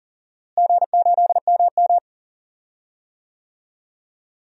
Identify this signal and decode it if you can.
Morse code, heard as a keyed tone, about 38 wpm, 700 Hz. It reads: G8MM
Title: DXCC-100 Call Signs - 4 Characters - Repeated 3x - 40wpm